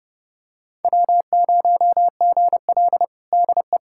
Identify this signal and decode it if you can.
Morse code, heard as a keyed tone, about 30 wpm, 700 Hz. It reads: W0GL DE
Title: USA Callsign DE Callsign - 4 Characters - 30wpm